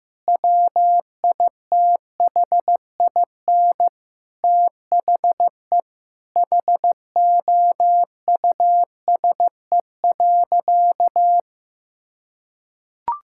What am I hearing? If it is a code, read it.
Morse code, heard as a keyed tone, about 15 wpm, 700 Hz. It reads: WITHIN THE HOUSE.
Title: Sentences from Top 1000 Words - Review Part 6 - 15wpm